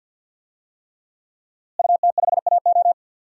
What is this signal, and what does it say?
Morse code, heard as a keyed tone, about 51 wpm, 700 Hz. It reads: UT5AO
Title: DXCC-100 Call Signs - 5 Characters - Repeated 3x - 50wpm